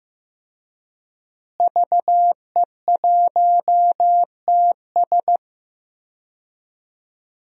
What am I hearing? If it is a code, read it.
Morse code, heard as a keyed tone, about 16 wpm, 700 Hz. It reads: VE1TS
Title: DXCC-100 Call Signs - 5 Characters - Repeated 3x - 15wpm